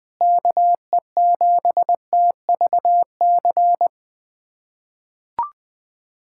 Morse code, heard as a keyed tone, about 20 wpm, 700 Hz. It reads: KE7TVC
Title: USA Callsign DE Callsign - 6 Characters - 20wpm